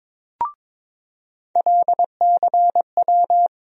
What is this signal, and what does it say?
Morse code, heard as a keyed tone, about 22 wpm, 700 Hz. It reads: LCW
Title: USA Call Sign Postfixes - 3 Letters 22wpm